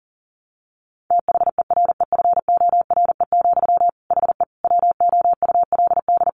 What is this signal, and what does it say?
Morse code, heard as a keyed tone, about 37 wpm, 700 Hz. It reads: THEREFORE, HE WOULD
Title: Sentences from Top 800 Words 40wpm